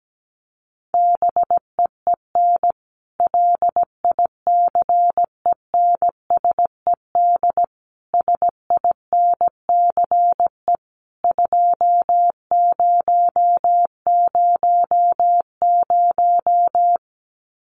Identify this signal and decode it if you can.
Morse code, heard as a keyed tone, about 17 wpm, 700 Hz. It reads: BEEN LICENSED SINCE 2000
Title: Common QSO Phrases 17wpm